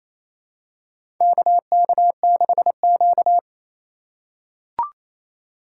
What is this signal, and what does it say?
Morse code, heard as a keyed tone, about 29 wpm, 700 Hz. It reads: KK6Q
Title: USA Call Signs - 4 Characters - 6x Character Spacing - 28wpm